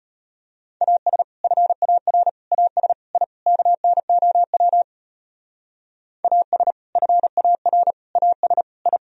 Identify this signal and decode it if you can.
Morse code, heard as a keyed tone, about 33 wpm, 700 Hz. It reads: AS FAR AS I KNOW AS FAR AS I